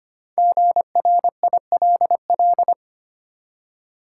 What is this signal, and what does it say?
Morse code, heard as a keyed tone, about 26 wpm, 690 Hz. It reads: GRILL
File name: practice / 25wpm / Easy Words with Hints - 25wpm